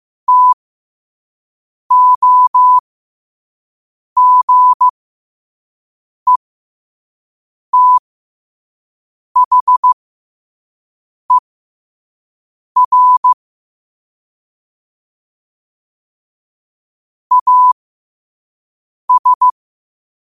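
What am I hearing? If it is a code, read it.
Morse code, heard as a keyed tone, about 15 wpm, spaced as for 5 wpm, 1000 Hz. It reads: TOGETHER AS
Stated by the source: 5 wpm Morse.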